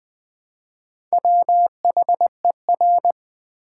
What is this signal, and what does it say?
Morse code, heard as a keyed tone, about 20 wpm, 700 Hz. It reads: WHER
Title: Build A Word - 2 Letters At A Time - 8-Letter Words - 20wpm